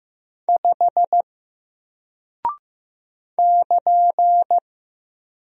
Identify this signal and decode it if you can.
Morse code, heard as a keyed tone, about 15 wpm, 700 Hz. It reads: 5 <KN>
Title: Single Character (Letter:Number:Punctuation:Prosign) 15wpm